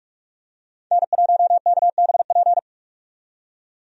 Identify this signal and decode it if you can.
Morse code, heard as a keyed tone, about 43 wpm, 700 Hz. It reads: N1KBP
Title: USA Call Signs - 5 Characters - 45wpm